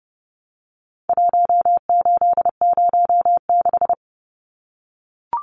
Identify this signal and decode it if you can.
Morse code, heard as a keyed tone, about 30 wpm, 700 Hz. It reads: 1806
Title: Numbers - 4-Digit 30wpm